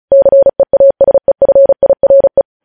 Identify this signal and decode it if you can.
Morse code, heard as a keyed tone, about 36 wpm, 550 Hz. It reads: CEASEFIRE
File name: podcast / News headlines-1x-35x35